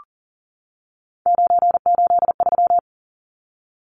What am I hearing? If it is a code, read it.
Morse code, heard as a keyed tone, about 40 wpm, 700 Hz. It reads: 983
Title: Numbers - 3-Digit 40wpm